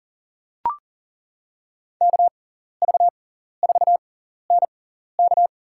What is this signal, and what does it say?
Morse code, heard as a keyed tone, about 37 wpm, spaced as for 14 wpm, 700 Hz. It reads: KV4NK
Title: USA Call Signs - 5 Characters - 6x Character Spacing - 40wpm